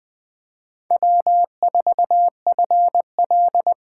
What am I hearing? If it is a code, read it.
Morse code, heard as a keyed tone, about 20 wpm, 700 Hz. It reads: W4FL
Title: Speed Racing - USA 1x2 Call Signs 17wpm